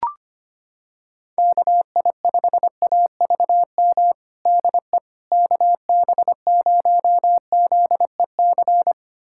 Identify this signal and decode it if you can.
Morse code, heard as a keyed tone, about 25 wpm, 700 Hz. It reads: KI5AVM DE KB0ZEC
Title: USA Callsign DE Callsign - 6 Characters - 25wpm